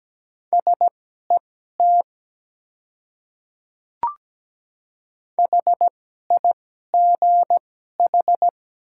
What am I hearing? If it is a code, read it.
Morse code, heard as a keyed tone, about 17 wpm, spaced as for 12 wpm, 700 Hz. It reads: SET HIGH
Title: 2x Character Spacing - Top 200 Words - 17wpm